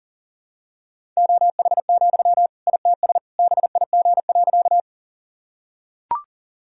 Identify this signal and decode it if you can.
Morse code, heard as a keyed tone, about 40 wpm, 700 Hz. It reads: OH, ITS BIG.
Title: Sentences from Top 1000 Words - Review Part 1 - 40wpm